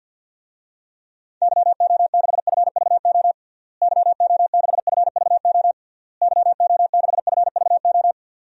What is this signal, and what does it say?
Morse code, heard as a keyed tone, about 53 wpm, 700 Hz. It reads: YO6FVO YO6FVO YO6FVO
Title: DXCC-100 Call Signs - 6 Characters - Repeated 3x - 50wpm